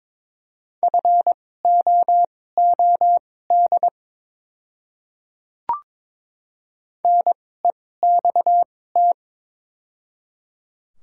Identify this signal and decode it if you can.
Morse code, heard as a keyed tone, about 22 wpm, spaced as for 16 wpm, 700 Hz. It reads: FOOD NEXT
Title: 2x Character Spacing - Top 500 Words - 22wpm